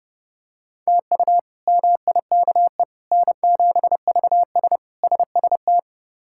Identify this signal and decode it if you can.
Morse code, heard as a keyed tone, about 30 wpm, 700 Hz. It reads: TU MIKE N7VS SST
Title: K1USN SST QSO Practice - 30wpm